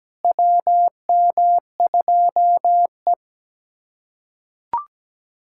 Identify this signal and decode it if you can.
Morse code, heard as a keyed tone, about 17 wpm, 700 Hz. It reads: WM2E
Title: USA Callsign DE Callsign - 4 Characters - 17wpm